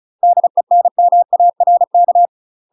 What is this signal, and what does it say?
Morse code, heard as a keyed tone, about 34 wpm, 700 Hz. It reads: DENMARK
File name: UK MASTER QSOs TRAINING / 35WPM / UK MASTER QSOs 0000